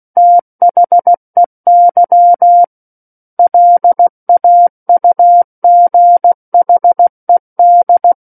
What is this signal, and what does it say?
Morse code, heard as a keyed tone, about 16 wpm, 700 Hz. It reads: THEY LAUGHED
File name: Book One / 07-Chapter 7 - 16 WPM